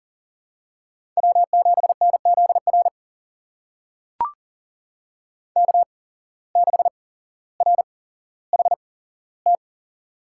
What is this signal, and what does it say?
Morse code, heard as a keyed tone, about 40 wpm, spaced as for 17 wpm, 700 Hz. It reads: W7NZR K6RHT
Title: USA Call Signs - 8x Character Spacing - 40wpm